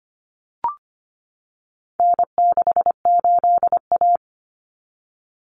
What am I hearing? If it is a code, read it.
Morse code, heard as a keyed tone, about 25 wpm, 700 Hz. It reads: N68A
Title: Common Cut Numbers - 4-Digit 25wpm